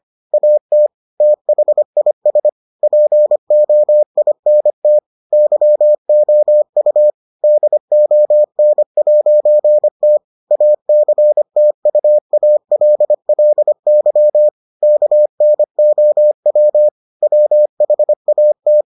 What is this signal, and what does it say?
Morse code, heard as a keyed tone, about 25 wpm, 600 Hz. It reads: AT THIS POINT YOU DON'T ACTUALLY KNOW WHAT